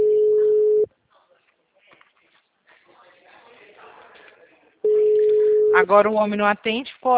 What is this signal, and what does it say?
Ringback tone (Europe, 425 Hz), with speech.
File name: out- -2001- - - .62874